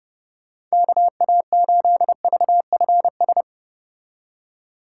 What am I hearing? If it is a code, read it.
Morse code, heard as a keyed tone, about 31 wpm, 700 Hz. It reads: KA8VFS